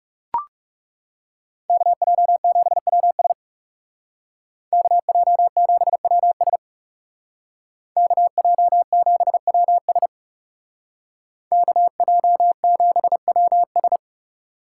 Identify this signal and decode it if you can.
Morse code, heard as a keyed tone, about 36 wpm, 700 Hz. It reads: KJ7WS KJ7WS KJ7WS KJ7WS
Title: Speed Racing - USA Call Signs 30wpm